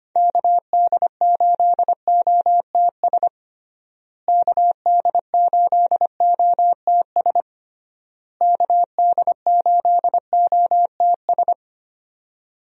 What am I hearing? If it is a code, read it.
Morse code, heard as a keyed tone, about 25 wpm, 700 Hz. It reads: KD8OTS KD8OTS KD8OTS
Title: USA Call Signs - 6 Characters - Repeated 3x - 25wpm